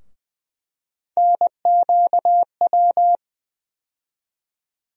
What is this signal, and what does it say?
Morse code, heard as a keyed tone, about 20 wpm, 700 Hz. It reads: NQW
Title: USA Call Sign Postfixes - 3 Letters 20wpm